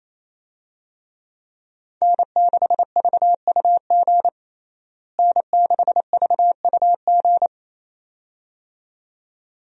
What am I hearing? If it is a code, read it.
Morse code, heard as a keyed tone, about 28 wpm, 700 Hz. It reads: N6VUG N6VUG
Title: USA Call Signs - 5 Characters - Repeated 2x - 28wpm